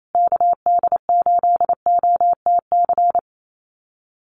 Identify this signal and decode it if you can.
Morse code, heard as a keyed tone, about 28 wpm, 700 Hz. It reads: KD8OTC
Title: USA Call Signs Repeated 3x 28wpm